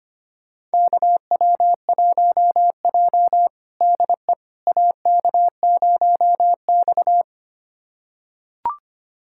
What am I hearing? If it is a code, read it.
Morse code, heard as a keyed tone, about 25 wpm, 700 Hz. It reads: KW1J DE AK0X
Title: USA Callsign DE Callsign - 4 Characters - 25wpm